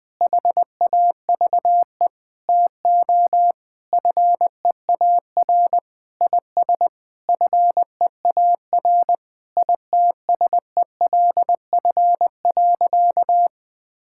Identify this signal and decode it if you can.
Morse code, heard as a keyed tone, about 21 wpm, 700 Hz. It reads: HAVE TO FEAR IS FEAR ITSELF.
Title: Sentences with 8-Word Phrases - 20wpm